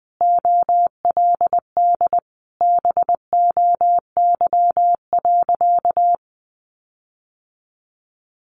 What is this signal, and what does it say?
Morse code, heard as a keyed tone, about 20 wpm, 700 Hz. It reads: OLD BOY.
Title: Sentences from Top 1500 Words 20wpm